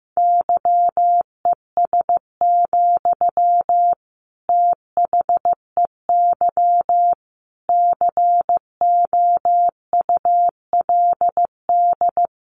Morse code, heard as a keyed tone, about 15 wpm, 700 Hz. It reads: YES, THEY COULD
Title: Sentences from Top 700 Words 15wpm